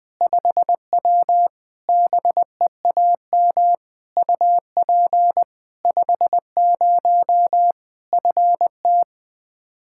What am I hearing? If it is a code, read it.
Morse code, heard as a keyed tone, about 20 wpm, 700 Hz. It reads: 5W BEAM UP 50 FT
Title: Practice QSO - Exchange 2 - Info - 20wpm